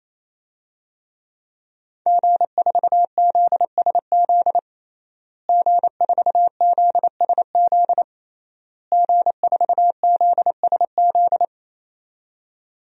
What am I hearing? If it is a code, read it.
Morse code, heard as a keyed tone, about 28 wpm, 700 Hz. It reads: G4ZSZ G4ZSZ G4ZSZ
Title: DXCC-100 Call Signs - 5 Characters - Repeated 3x - 28wpm